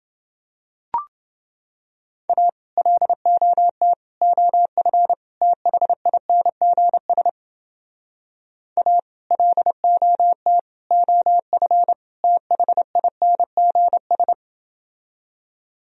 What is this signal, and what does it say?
Morse code, heard as a keyed tone, about 30 wpm, 700 Hz. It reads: A LOT OF THINGS A LOT OF THINGS